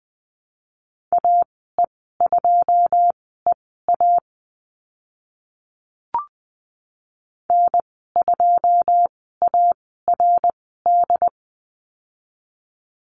Morse code, heard as a keyed tone, about 20 wpm, spaced as for 15 wpm, 700 Hz. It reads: AE2EA N2ARD
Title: 082 Lesson 14g - Number 2 - Call Signs 20wpm